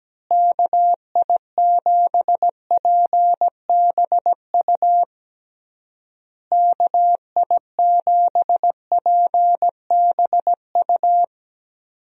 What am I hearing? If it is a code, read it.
Morse code, heard as a keyed tone, about 17 wpm, 700 Hz. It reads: KI7PBU KI7PBU
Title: USA Call Signs - 6 Characters - Repeated 3x - 17wpm